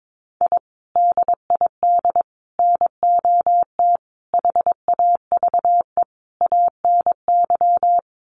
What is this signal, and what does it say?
Morse code, heard as a keyed tone, about 22 wpm, 700 Hz. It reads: I DID NOT HAVE ANY
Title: Sets of 5 Words Encore 22wpm